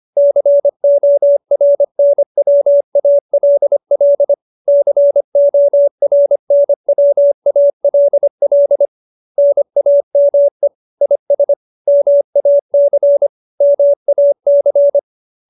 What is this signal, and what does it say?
Morse code, heard as a keyed tone, about 25 wpm, 570 Hz. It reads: CORNWALL CORNWALL NAME IS MAC MAC
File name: QSO310 25